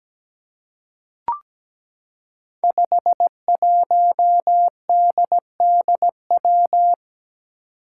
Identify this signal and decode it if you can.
Morse code, heard as a keyed tone, about 17 wpm, 700 Hz. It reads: 51DDW
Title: Code Groups - 5 Letter:Numbers - 17wpm